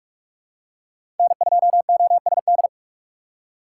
Morse code, heard as a keyed tone, about 45 wpm, 700 Hz. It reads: N2OSD
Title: USA Call Signs - 5 Characters - 3x Character Spacing - 45wpm